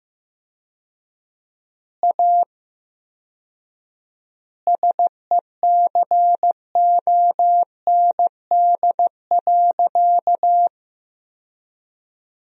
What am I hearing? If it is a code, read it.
Morse code, heard as a keyed tone, about 16 wpm, 700 Hz. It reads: A SECOND.
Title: 4x Word-Spacing - Sentences from Top 500 Words - 15wpm